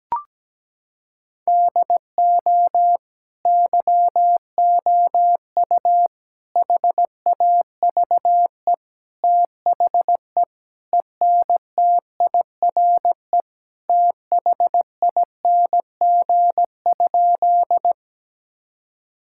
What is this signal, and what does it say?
Morse code, heard as a keyed tone, about 17 wpm, 700 Hz. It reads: DO YOU HAVE THE ENTIRE THING?
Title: Sentences from Top 1000 Words - Review Part 5 - 17wpm